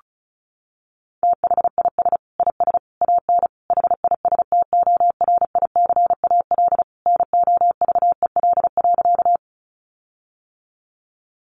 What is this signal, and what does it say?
Morse code, heard as a keyed tone, about 34 wpm, 700 Hz. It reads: THIS IS AN HISTORICAL NOVEL.
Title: Sentences from Top 1300 Words 35wpm